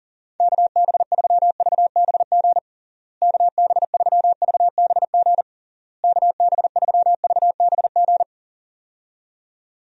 Morse code, heard as a keyed tone, about 40 wpm, 700 Hz. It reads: KB3VBG KB3VBG KB3VBG
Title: USA Call Signs - 6 Characters - Repeated 3x - 40wpm